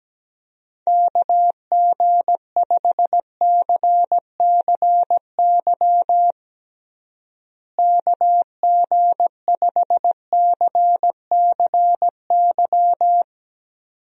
Morse code, heard as a keyed tone, about 17 wpm, 700 Hz. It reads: KG5CCY KG5CCY
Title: USA Call Signs - 6 Characters - Repeated 2x - 17wpm